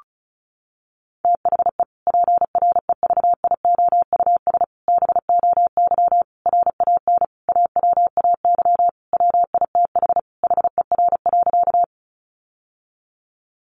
Morse code, heard as a keyed tone, about 34 wpm, 700 Hz. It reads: THE PREVIOUS BOY RAN AWAY WITH HER.
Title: Sentences from Top 1500 Words 35wpm